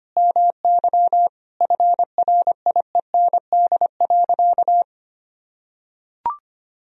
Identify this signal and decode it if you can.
Morse code, heard as a keyed tone, about 25 wpm, 700 Hz. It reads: MY FRIEND.
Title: Sentences from Top 300 Words and Abbreviations 25wpm